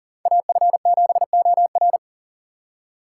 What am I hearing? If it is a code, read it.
Morse code, heard as a keyed tone, about 38 wpm, 690 Hz. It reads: AF7OR
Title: USA Call Signs - 2x Character Spacing - 40wpm